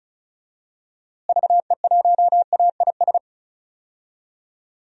Morse code, heard as a keyed tone, about 36 wpm, 700 Hz. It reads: VE1AIS
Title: DXCC-100 Call Signs - 6 Characters - Repeated 2x - 35wpm